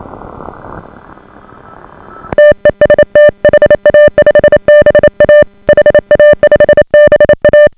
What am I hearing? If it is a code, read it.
Morse code, heard as a keyed tone, about 27 wpm, 590 Hz. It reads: TEST HA5BA HA5BA